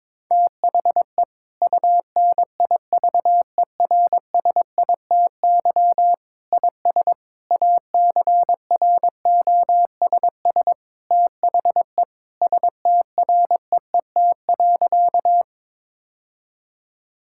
Morse code, heard as a keyed tone, about 22 wpm, 700 Hz. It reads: THE UNIVERSITY IS ACROSS THE STREET.
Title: Sentences from Top 1000 Words - Review Part 5 - 22wpm